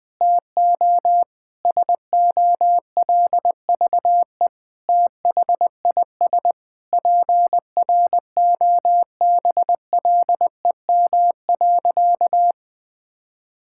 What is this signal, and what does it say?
Morse code, heard as a keyed tone, about 21 wpm, 700 Hz. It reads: TO SOLVE THIS PROBLEM.
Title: Sentences with 7-Word Phrases - 20wpm